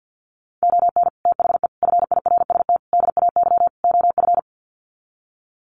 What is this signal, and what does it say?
Morse code, heard as a keyed tone, about 50 wpm, 700 Hz. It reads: ON THE FIRST DAY OF
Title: Sets of 5 Words Encore 50wpm